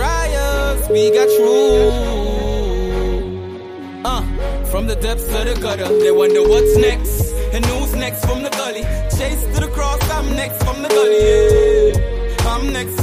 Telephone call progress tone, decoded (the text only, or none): ringback tone